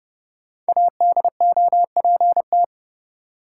Morse code, heard as a keyed tone, about 30 wpm, 700 Hz. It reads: ADOPT
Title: Five Letter Words - 30wpm